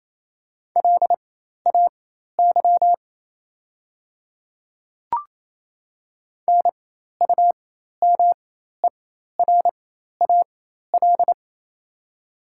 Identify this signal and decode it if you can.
Morse code, heard as a keyed tone, about 28 wpm, spaced as for 13 wpm, 690 Hz. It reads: LAY NUMERAL
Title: 4x Character Spacing - Top 500 Words - 28wpm